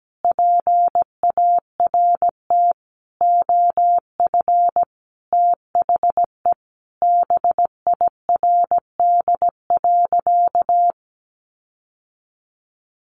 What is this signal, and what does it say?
Morse code, heard as a keyed tone, about 17 wpm, 700 Hz. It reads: PART OF THE BIRD.
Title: Sentences from Top 1500 Words 17wpm